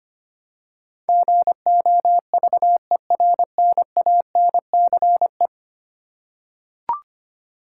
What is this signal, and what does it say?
Morse code, heard as a keyed tone, about 25 wpm, 700 Hz. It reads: GOVERNANCE
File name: 10 Letter Words - First 3 Letters Given - 25wpm